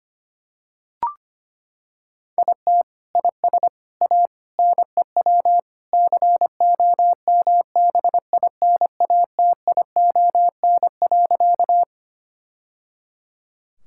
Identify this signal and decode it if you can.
Morse code, heard as a keyed tone, about 25 wpm, 700 Hz. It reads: IT IS A NEW COMBINATION.
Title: Sentences from Top 1000 Words - Review Part 6 - 25wpm